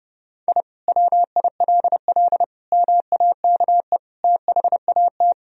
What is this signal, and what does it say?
Morse code, heard as a keyed tone, about 30 wpm, 700 Hz. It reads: I WILL MAKE THAT
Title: Sentences from Top 1000 Words - Review Part 3 - 30wpm